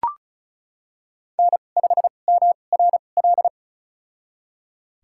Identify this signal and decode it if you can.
Morse code, heard as a keyed tone, about 36 wpm, spaced as for 25 wpm, 700 Hz. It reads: N5MRL
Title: USA Call Signs - 5 Characters - 2x Character Spacing - 35wpm